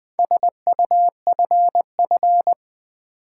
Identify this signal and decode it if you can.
Morse code, heard as a keyed tone, about 20 wpm, 700 Hz. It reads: SUFF
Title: Build A Word - 4 Letters At A Time - 10-Letter Words - 20wpm